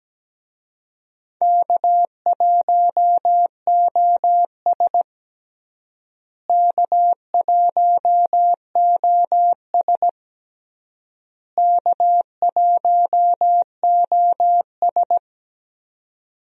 Morse code, heard as a keyed tone, about 17 wpm, 700 Hz. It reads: K1OS K1OS K1OS
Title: USA Call Signs - 4 Characters - Repeated 3x - 17wpm